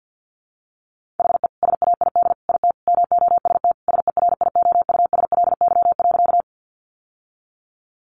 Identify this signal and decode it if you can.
Morse code, heard as a keyed tone, about 50 wpm, 700 Hz. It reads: HE SAID IT MOST SERIOUSLY.
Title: Sentences from Top 1200 Words 50wpm